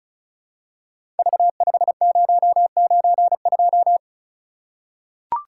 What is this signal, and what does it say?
Morse code, heard as a keyed tone, about 36 wpm, 700 Hz. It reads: V5092
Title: Code Groups - 5 Letter:Numbers - 35wpm